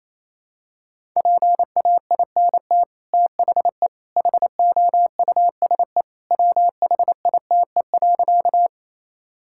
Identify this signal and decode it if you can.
Morse code, heard as a keyed tone, about 28 wpm, 700 Hz. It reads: PAINT THE HOUSE WHITE.
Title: Sentences from Top 1000 Words - Review Part 6 - 28wpm